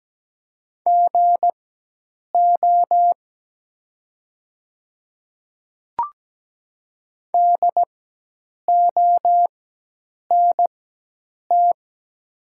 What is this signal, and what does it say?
Morse code, heard as a keyed tone, about 17 wpm, spaced as for 8 wpm, 700 Hz. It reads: GO DONT
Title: 4x Character Spacing - Top 500 Words - 17wpm